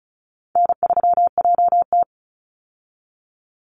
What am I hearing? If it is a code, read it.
Morse code, heard as a keyed tone, about 34 wpm, 700 Hz. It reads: N3JT
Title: Easy Call Signs Repeated 3x - 35wpm